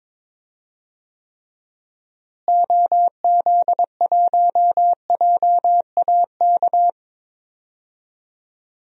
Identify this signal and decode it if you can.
Morse code, heard as a keyed tone, about 22 wpm, 700 Hz. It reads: OZ1JAK
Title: DXCC-100 Call Signs - 6 Characters - Repeated 2x - 22wpm